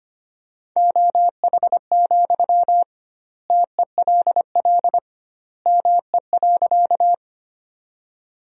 Morse code, heard as a keyed tone, about 25 wpm, 700 Hz. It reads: OH, TELL ME.
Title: 2x Word-Spacing - Sentences from Top 500 Words - 25wpm